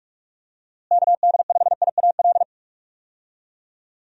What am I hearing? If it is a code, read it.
Morse code, heard as a keyed tone, about 46 wpm, 700 Hz. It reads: KD5IAL
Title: USA Call Signs - 6 Characters - Repeated 2x - 45wpm